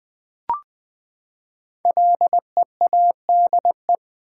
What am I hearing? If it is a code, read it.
Morse code, heard as a keyed tone, about 20 wpm, 700 Hz. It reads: LEADE
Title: Build A Word - 5 Letters At A Time - 10-Letter Words - 20wpm